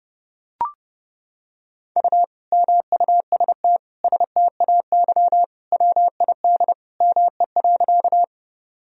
Morse code, heard as a keyed tone, about 28 wpm, 700 Hz. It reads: U MUST STAY WID ME.
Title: Sentences from Top 500 Words and Abbreviations 30wpm